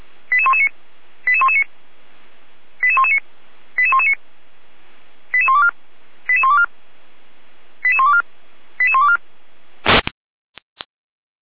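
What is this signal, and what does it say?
Five-tone selective call (ZVEI), heard as quick runs of five tones, tones of about 70 ms. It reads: 88119 88119 88119 88119 88124 88124 88124 88124
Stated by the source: ZVEI five-tone call received with an SDR.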